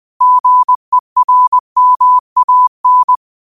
Morse code, heard as a keyed tone, about 20 wpm, 1000 Hz. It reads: GERMAN